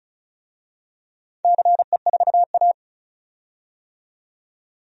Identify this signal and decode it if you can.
Morse code, heard as a keyed tone, about 34 wpm, 700 Hz. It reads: CE4A
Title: DXCC-100 Call Signs - 4 Characters - 35wpm